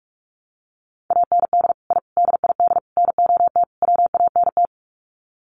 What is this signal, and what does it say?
Morse code, heard as a keyed tone, about 44 wpm, 700 Hz. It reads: AND I DID NOT WANT
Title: Sets of 5 Words 45wpm